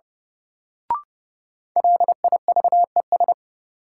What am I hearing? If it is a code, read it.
Morse code, heard as a keyed tone, about 30 wpm, 700 Hz. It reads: LIVES
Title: Sets of 5 Words Encore - New Word Drill - 30wpm